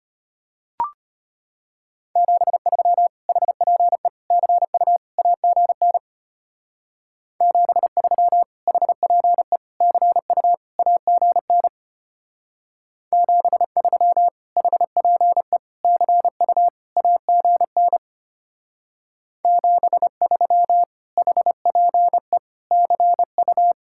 Morse code, heard as a keyed tone, about 32 wpm, 700 Hz. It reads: 73 HPE CU AGN 73 HPE CU AGN 73 HPE CU AGN 73 HPE CU